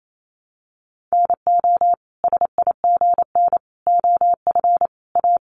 Morse code, heard as a keyed tone, about 28 wpm, 700 Hz. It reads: NO SIGN OF A
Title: Sentences from Top 1000 Words 28wpm